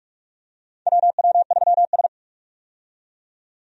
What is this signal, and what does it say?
Morse code, heard as a keyed tone, about 46 wpm, 700 Hz. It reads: WW3S
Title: USA Call Signs - 4 Characters - 45wpm